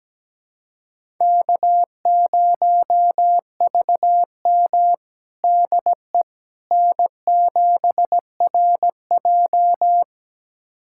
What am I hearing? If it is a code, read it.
Morse code, heard as a keyed tone, about 17 wpm, 700 Hz. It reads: K0VM DE N7RJ
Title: USA Callsign DE Callsign - 4 Characters - 17wpm